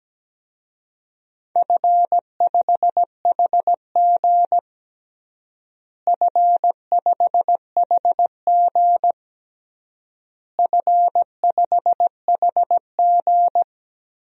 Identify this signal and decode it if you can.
Morse code, heard as a keyed tone, about 17 wpm, 700 Hz. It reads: F5HG F5HG F5HG
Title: DXCC-100 Call Signs - 4 Characters - Repeated 3x - 17wpm